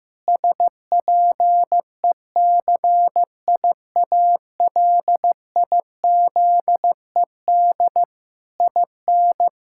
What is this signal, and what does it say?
Morse code, heard as a keyed tone, about 16 wpm, 700 Hz. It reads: SPECIALIZED IN
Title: Pangram Challenge v2 - 15wpm